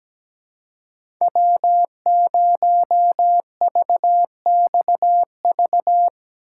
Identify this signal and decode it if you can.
Morse code, heard as a keyed tone, about 17 wpm, 700 Hz. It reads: W0VXV
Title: USA Call Signs - 5 Characters - 2x Character Spacing - 17wpm